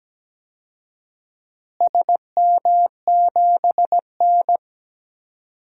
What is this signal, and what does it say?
Morse code, heard as a keyed tone, about 17 wpm, 700 Hz. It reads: SM7N
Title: DXCC-100 Call Signs - 4 Characters - Repeated 2x - 17wpm